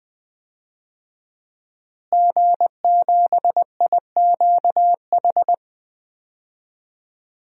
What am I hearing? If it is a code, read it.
Morse code, heard as a keyed tone, about 20 wpm, 700 Hz. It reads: G7IQH
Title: DXCC-100 Call Signs - 5 Characters - Repeated 2x - 20wpm